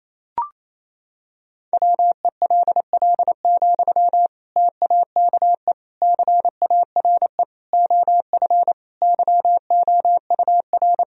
Morse code, heard as a keyed tone, about 28 wpm, 700 Hz. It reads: WELL, TAKE CARE OF YOUR
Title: Sentences from Top 1000 Words - Review Part 1 - 28wpm